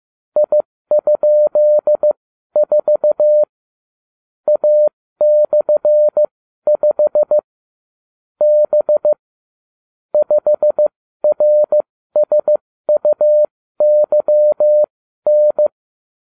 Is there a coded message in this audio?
Yes: Morse code from 0.4 to 15.7 s, heard as a keyed tone, about 15 wpm, spaced as for 12 wpm, 600 Hz. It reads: I?4 A/5 B 5RSUYN